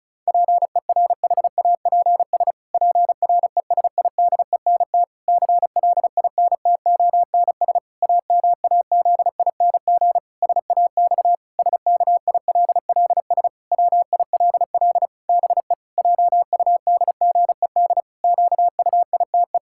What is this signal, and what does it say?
Morse code, heard as a keyed tone, about 36 wpm, 700 Hz. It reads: PERHAPS PRESIDENT CLINTONS AMAZING SAX SKILLS WILL BE JUDGED QUITE
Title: Pangram Challenge v2 - 35wpm